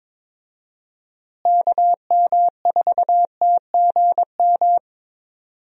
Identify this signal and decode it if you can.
Morse code, heard as a keyed tone, about 22 wpm, 700 Hz. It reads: KM4TGM
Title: USA Call Signs - 6 Characters - Repeated 2x - 22wpm